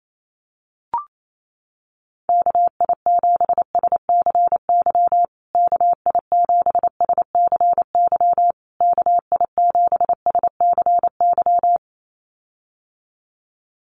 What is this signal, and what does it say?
Morse code, heard as a keyed tone, about 28 wpm, 700 Hz. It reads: KI7SCY KI7SCY KI7SCY
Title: USA Call Signs Repeated 3x 28wpm